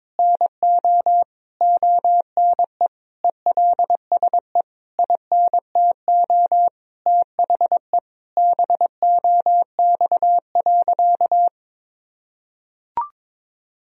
Morse code, heard as a keyed tone, about 22 wpm, 700 Hz. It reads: NO ONE ELSE INTO THE BOX.
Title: Sentences from Top 1000 Words - Review Part 2 - 22wpm